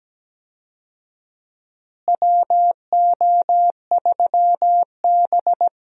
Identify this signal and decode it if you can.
Morse code, heard as a keyed tone, about 17 wpm, 700 Hz. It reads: WO3B
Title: USA Call Signs - 4 Characters - Repeated 3x - 17wpm